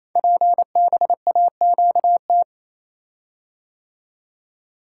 Morse code, heard as a keyed tone, about 27 wpm, 700 Hz. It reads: PBAQT
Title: Code Groups - 5 Letters - 28wpm